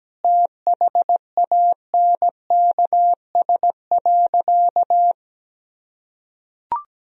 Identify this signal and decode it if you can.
Morse code, heard as a keyed tone, about 17 wpm, 700 Hz. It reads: THANKS.
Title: Sentences from Top 1000 Words - Review Part 3 - 17wpm